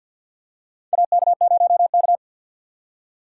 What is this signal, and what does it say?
Morse code, heard as a keyed tone, about 50 wpm, 700 Hz. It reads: AK0K
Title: USA Call Signs - 4 Characters - 6x Character Spacing - 50wpm